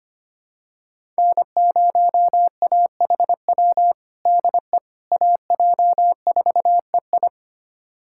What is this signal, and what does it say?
Morse code, heard as a keyed tone, about 25 wpm, 700 Hz. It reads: N0AHW DE AJ4EI
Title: USA Callsign DE Callsign - 5 Characters - 25wpm